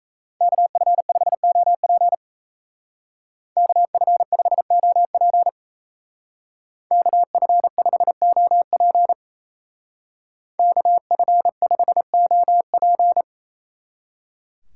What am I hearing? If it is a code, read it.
Morse code, heard as a keyed tone, about 33 wpm, 700 Hz. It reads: KF5OP KF5OP KF5OP KF5OP
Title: Speed Racing - USA 2x2 Call Signs 28wpm